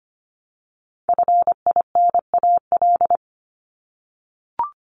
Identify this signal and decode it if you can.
Morse code, heard as a keyed tone, about 25 wpm, 700 Hz. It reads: FINAL
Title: Top 500 Words Repeated 25wpm